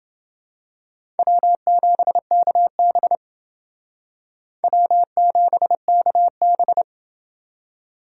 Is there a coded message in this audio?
Yes: Morse code from 1.2 to 6.8 s, heard as a keyed tone, about 29 wpm, 700 Hz. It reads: W7KB W7KB